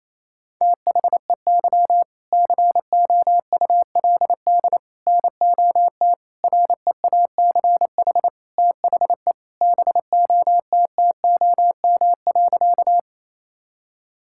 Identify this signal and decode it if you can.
Morse code, heard as a keyed tone, about 27 wpm, 700 Hz. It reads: THEY COULD NOT REACH THE BOTTOM.
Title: Sentences from Top 1000 Words - Review Part 6 - 28wpm